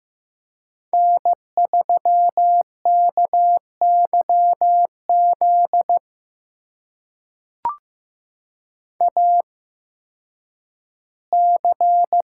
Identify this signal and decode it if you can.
Morse code, heard as a keyed tone, about 15 wpm, spaced as for 10 wpm, 700 Hz. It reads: N3KYZ AC
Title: USA Call Signs - 5 Characters - 8x Character Spacing - 15wpm